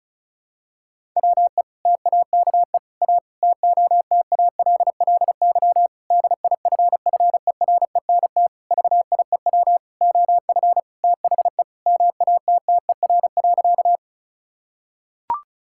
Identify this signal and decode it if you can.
Morse code, heard as a keyed tone, about 36 wpm, 700 Hz. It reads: WE TAKE A TOTALLY DIFFERENT VIEW OF THE MATTER.
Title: Sentences from Top 1600 Words 35wpm